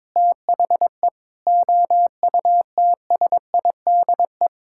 Morse code, heard as a keyed tone, about 22 wpm, 700 Hz. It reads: THE OUTSIDE
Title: Sentences from Top 1000 Words - Review Part 6 - 22wpm